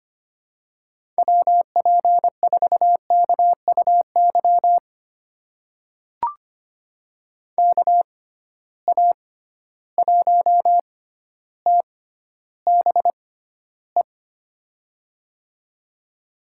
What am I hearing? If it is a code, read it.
Morse code, heard as a keyed tone, about 25 wpm, spaced as for 13 wpm, 700 Hz. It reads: WP4KUY KA1TBE
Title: USA Call Signs - 6 Characters - 6x Character Spacing - 25wpm